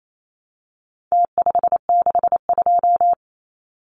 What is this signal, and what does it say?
Morse code, heard as a keyed tone, about 28 wpm, 700 Hz. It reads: T562
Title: Numbers - Contest Style - 4-Digit 28wpm